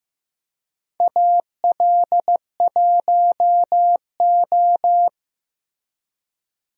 Morse code, heard as a keyed tone, about 15 wpm, 700 Hz. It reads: AL1O
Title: USA Call Signs - 4 Characters - 15wpm